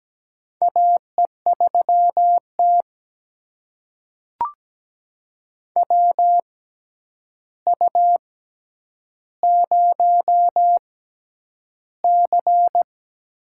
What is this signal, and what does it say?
Morse code, heard as a keyed tone, about 17 wpm, spaced as for 9 wpm, 700 Hz. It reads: AE3T WU0C
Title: USA Call Signs - 4 Characters - 6x Character Spacing - 17wpm